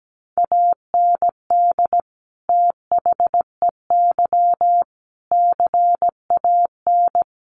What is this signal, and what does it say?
Morse code, heard as a keyed tone, about 17 wpm, 700 Hz. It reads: AND THEY CAN
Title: Sentences from Top 1100 Words 17wpm